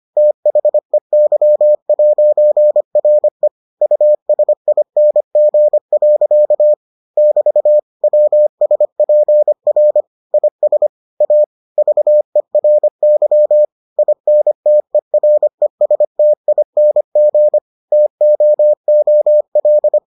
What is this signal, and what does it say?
Morse code, heard as a keyed tone, about 25 wpm, 600 Hz. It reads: THEY'RE USING. <BT> WSPR IS A VERY INTERESTING TOOL